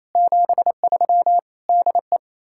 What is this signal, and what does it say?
Morse code, heard as a keyed tone, about 37 wpm, 700 Hz. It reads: 73 DE